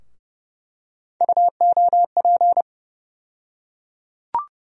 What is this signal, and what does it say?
Morse code, heard as a keyed tone, about 29 wpm, 700 Hz. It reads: UOP
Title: USA Call Sign Postfixes - 3 Letters 30wpm